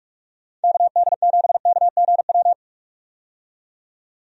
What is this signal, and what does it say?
Morse code, heard as a keyed tone, about 46 wpm, 700 Hz. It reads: KD7KGW
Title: USA Call Signs - 6 Characters - 45wpm